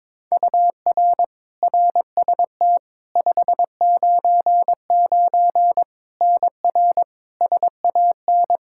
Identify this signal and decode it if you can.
Morse code, heard as a keyed tone, about 22 wpm, 700 Hz. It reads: UR RST 599 NR SAN
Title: Practice QSO - Exchange 1 - Beginning - 22wpm